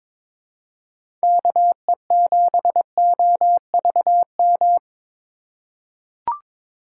Morse code, heard as a keyed tone, about 22 wpm, 700 Hz. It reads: KE7OVM
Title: USA Call Signs - 6 Characters - 4x Character Spacing - 22wpm